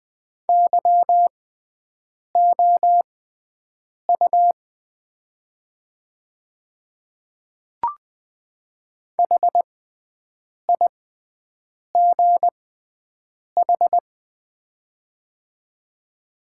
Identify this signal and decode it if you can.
Morse code, heard as a keyed tone, about 20 wpm, spaced as for 7 wpm, 700 Hz. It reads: YOU HIGH
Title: 6x Character Spacing - Top 300 Words - 20wpm